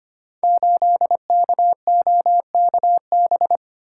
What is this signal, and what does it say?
Morse code, heard as a keyed tone, about 25 wpm, 700 Hz. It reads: 8KOKB
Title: Code Groups - 5 Letter:Numbers - 25wpm